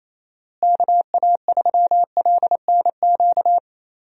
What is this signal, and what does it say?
Morse code, heard as a keyed tone, about 28 wpm, 700 Hz. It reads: KA3LNQ
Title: USA Call Signs - 6 Characters - Repeated 2x - 28wpm